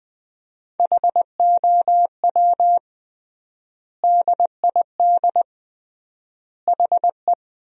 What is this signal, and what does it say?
Morse code, heard as a keyed tone, about 20 wpm, 700 Hz. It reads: HOW DID HE
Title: 3x Word-Spacing - Sentences from Top 100 Words - 20wpm